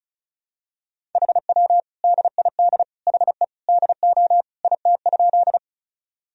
Morse code, heard as a keyed tone, about 34 wpm, 700 Hz. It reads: HW DID HE DO IT?
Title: Sentences from Top 100 Words and Abbreviations 35wpm